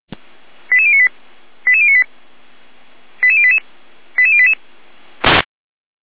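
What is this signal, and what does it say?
Five-tone selective call (ZVEI), heard as quick runs of five tones, tones of about 70 ms. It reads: 88098 88098 88088 88088
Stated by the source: ZVEI five-tone call 88571 received by SDR.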